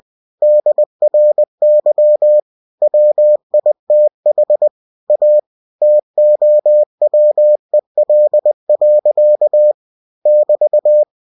Morse code, heard as a keyed tone, about 20 wpm, 600 Hz. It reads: DRY WITH A TOWEL. <BT>
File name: cw-books / winnie-the-pooh / winnie-20wpm-page-0072